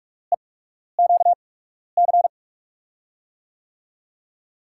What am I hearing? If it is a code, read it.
Morse code, heard as a keyed tone, about 44 wpm, spaced as for 12 wpm, 700 Hz. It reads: EQC
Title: USA Call Signs - 6 Characters - 8x Character Spacing - 45wpm